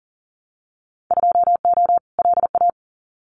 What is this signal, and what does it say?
Morse code, heard as a keyed tone, about 40 wpm, 700 Hz. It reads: 2O LA
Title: Winter Field Day Exchanges 40wpm